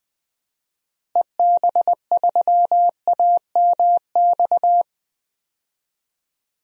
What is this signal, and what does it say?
Morse code, heard as a keyed tone, about 20 wpm, 700 Hz. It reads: EB3AMX
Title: DXCC-100 Call Signs - 6 Characters - Repeated 3x - 20wpm